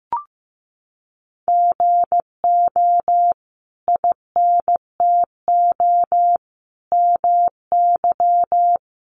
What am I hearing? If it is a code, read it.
Morse code, heard as a keyed tone, about 15 wpm, 700 Hz. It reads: GO INTO MY
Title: Sentences from Top 1100 Words 15wpm